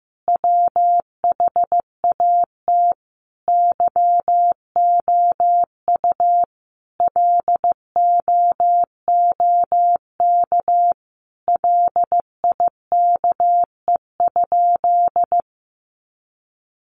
Morse code, heard as a keyed tone, about 15 wpm, 700 Hz. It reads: WHAT YOU LOOK LIKE?
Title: Sentences from Top 100 Words 15wpm